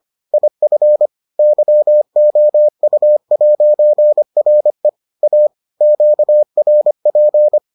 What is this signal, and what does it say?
Morse code, heard as a keyed tone, about 25 wpm, 600 Hz. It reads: IF YOU'RE A QRP